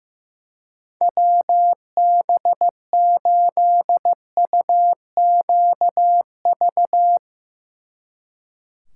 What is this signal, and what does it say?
Morse code, heard as a keyed tone, about 15 wpm, 700 Hz. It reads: WB8UQV
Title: Speed Racing - USA 2x3 Call Signs 15wpm